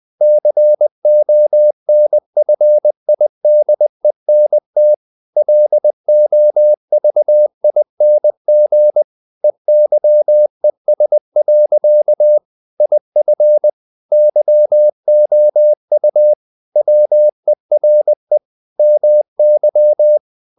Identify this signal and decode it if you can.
Morse code, heard as a keyed tone, about 20 wpm, 600 Hz. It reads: CONFIDENT LOVING EYES. IF YOU WERE MY
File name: the old man and the sea / the old man and the sea part 1 20wpm